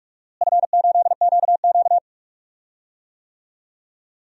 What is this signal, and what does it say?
Morse code, heard as a keyed tone, about 43 wpm, 700 Hz. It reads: F8QQ
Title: DXCC-100 Call Signs - 4 Characters - Repeated 3x - 45wpm